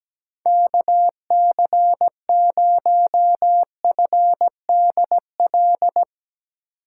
Morse code, heard as a keyed tone, about 17 wpm, 700 Hz. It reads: KC0FDL
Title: USA Call Signs - 6 Characters - 2x Character Spacing - 17wpm